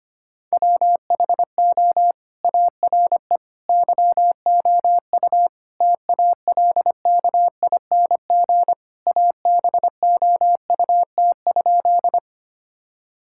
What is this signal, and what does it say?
Morse code, heard as a keyed tone, about 25 wpm, 700 Hz. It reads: WHO ARE YOU TALKING ABOUT?
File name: practice / 25wpm / Sentences with 2-Word Phrases - 25wpm